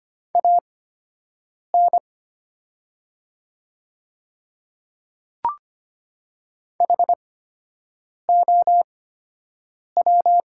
Morse code, heard as a keyed tone, about 25 wpm, spaced as for 7 wpm, 700 Hz. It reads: AN HOW
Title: 8x Character Spacing - Top 300 Words - 25wpm